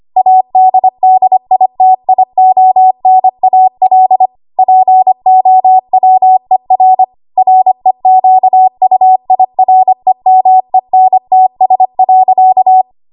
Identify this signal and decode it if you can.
Morse code, heard as a keyed tone, about 25 wpm, 750 Hz. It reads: ADDITIONAL POWER REQUIREMENTS.
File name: Archive / 25 WPM / 25WPM